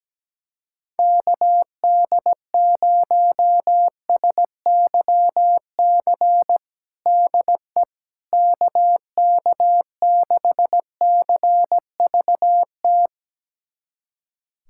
Morse code, heard as a keyed tone, about 17 wpm, 700 Hz. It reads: KD0SYC DE KK6CVT
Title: USA Callsign DE Callsign 17wpm